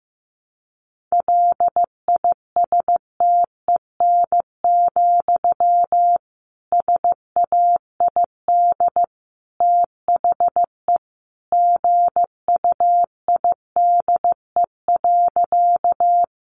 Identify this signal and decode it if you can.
Morse code, heard as a keyed tone, about 15 wpm, 700 Hz. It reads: LISTEN, SAID THE GUIDE.
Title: Sentences from Top 600 Words 15wpm